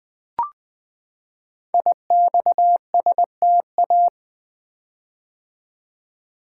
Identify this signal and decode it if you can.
Morse code, heard as a keyed tone, about 20 wpm, 700 Hz. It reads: IXSTA
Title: Code Groups - 5 Letters - 20wpm